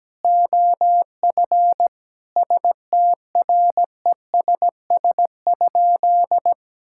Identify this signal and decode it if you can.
Morse code, heard as a keyed tone, about 17 wpm, 700 Hz. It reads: OF STRESS?
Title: Sentences from Top 1000 Words - Review Part 6 - 17wpm